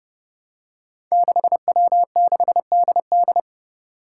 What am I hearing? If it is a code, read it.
Morse code, heard as a keyed tone, about 30 wpm, 700 Hz. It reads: 6W6DD
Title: Five Challenging Letter-Number Combinations 30wpm